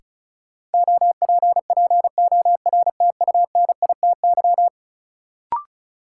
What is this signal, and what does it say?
Morse code, heard as a keyed tone, about 36 wpm, 700 Hz. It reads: OPPORTUNITY
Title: Sentences from Top 1000 Words - New Word Drill - 35wpm